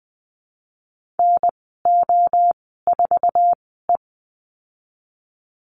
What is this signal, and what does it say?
Morse code, heard as a keyed tone, about 20 wpm, spaced as for 15 wpm, 700 Hz. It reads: NO4E
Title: 048 Lesson 9g - Number 4 - Call Signs 20wpm